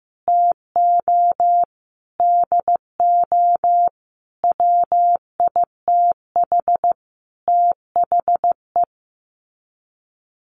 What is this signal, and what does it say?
Morse code, heard as a keyed tone, about 15 wpm, 700 Hz. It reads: TO DO WITH THE
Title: Sets of 5 Words Encore 15wpm